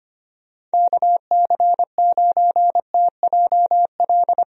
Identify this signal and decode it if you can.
Morse code, heard as a keyed tone, about 25 wpm, 700 Hz. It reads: KC9TJL
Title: USA Call Signs - 6 Characters - Repeated 3x - 25wpm